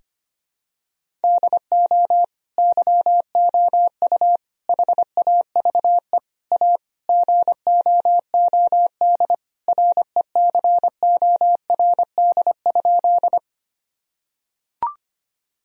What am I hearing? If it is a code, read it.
Morse code, heard as a keyed tone, about 25 wpm, 700 Hz. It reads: DO YOU HAVE A GOOD RECORD?
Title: Sentences from Top 1000 Words - Review Part 3 - 25wpm